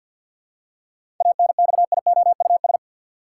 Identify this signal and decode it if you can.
Morse code, heard as a keyed tone, about 46 wpm, 700 Hz. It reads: ANXIOUS
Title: Seven Letter Words - 50wpm